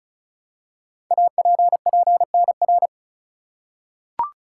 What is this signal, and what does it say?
Morse code, heard as a keyed tone, about 36 wpm, 700 Hz. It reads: APPNR
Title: Code Groups - 5 Letter:Numbers - 35wpm